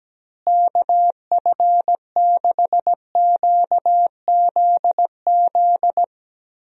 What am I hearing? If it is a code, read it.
Morse code, heard as a keyed tone, about 17 wpm, 700 Hz. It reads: KF6QZZ
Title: USA Call Signs - 6 Characters - Repeated 2x - 17wpm